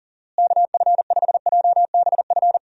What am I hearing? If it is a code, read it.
Morse code, heard as a keyed tone, about 40 wpm, 700 Hz. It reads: KF5JBF
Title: USA Call Signs - 6 Characters - Repeated 2x - 40wpm